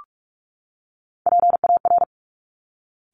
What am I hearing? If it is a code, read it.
Morse code, heard as a keyed tone, about 46 wpm, 700 Hz. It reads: PAR
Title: Top Three Letter Combinations in Words 45wpm